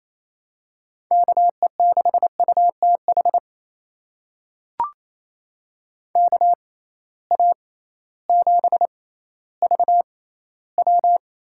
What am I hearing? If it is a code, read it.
Morse code, heard as a keyed tone, about 28 wpm, spaced as for 15 wpm, 700 Hz. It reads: KE6UTH KA7VW
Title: USA Call Signs - 6 Characters - 6x Character Spacing - 28wpm